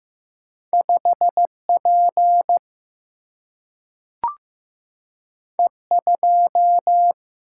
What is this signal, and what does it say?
Morse code, heard as a keyed tone, about 15 wpm, 700 Hz. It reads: 5P E2
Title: Single-Letter Single-Number 15wpm